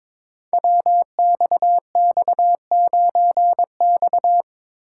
Morse code, heard as a keyed tone, about 22 wpm, 700 Hz. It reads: WXX9X
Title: Code Groups - 5 Letter:Numbers - 22wpm